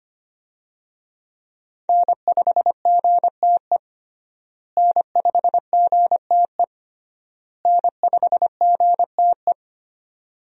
Morse code, heard as a keyed tone, about 25 wpm, 700 Hz. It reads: N5GTE N5GTE N5GTE
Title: USA Call Signs - 5 Characters - Repeated 3x - 25wpm